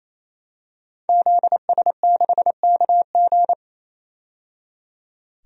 Morse code, heard as a keyed tone, about 28 wpm, 700 Hz. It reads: ZS6KG
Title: DXCC-100 Call Signs - 5 Characters - 28wpm